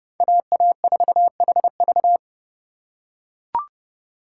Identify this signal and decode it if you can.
Morse code, heard as a keyed tone, about 32 wpm, 700 Hz. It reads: AA4HV
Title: K1USN SST QSO Practice - 30wpm